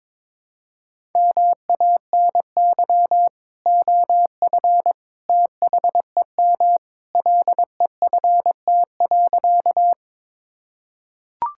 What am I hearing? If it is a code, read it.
Morse code, heard as a keyed tone, about 22 wpm, 700 Hz. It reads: MANY OF THEM LEFT.
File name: practice / 22wpm / Sentences with 2-Word Phrases - 22wpm